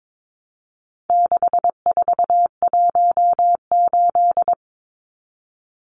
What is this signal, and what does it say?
Morse code, heard as a keyed tone, about 22 wpm, 700 Hz. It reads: 6418
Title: Numbers - 4-Digit 22wpm